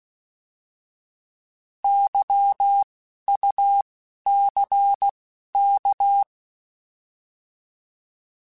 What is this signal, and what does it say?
Morse code, heard as a keyed tone, about 16 wpm, 800 Hz. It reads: Y U C K
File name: Alphabetical / Alphabetical mp3s / Ys @10